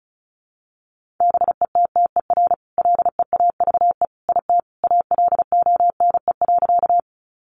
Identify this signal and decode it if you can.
Morse code, heard as a keyed tone, about 34 wpm, 700 Hz. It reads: BETTER LEAVE IT ALONE.
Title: Sentences from Top 1000 Words 35wpm